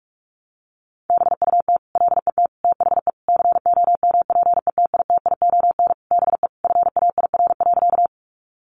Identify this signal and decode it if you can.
Morse code, heard as a keyed tone, about 44 wpm, 700 Hz. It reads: BUT LET THE COMPETITION BE FAIR.
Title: Sentences from Top 1300 Words 45wpm